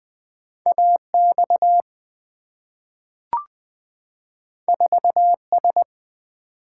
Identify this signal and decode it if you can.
Morse code, heard as a keyed tone, about 20 wpm, 700 Hz. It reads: A X 4 S
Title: Two Letter-Number Combinations - 20wpm